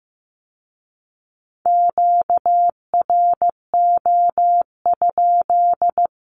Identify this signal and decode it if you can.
Morse code, heard as a keyed tone, about 15 wpm, 700 Hz. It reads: QRO?
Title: Q-Codes with Spoken Meaning - 15wpm